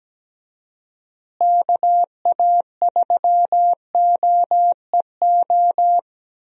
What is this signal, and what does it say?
Morse code, heard as a keyed tone, about 17 wpm, 700 Hz. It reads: KA3OEO
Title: USA Call Signs - 6 Characters - Repeated 2x - 17wpm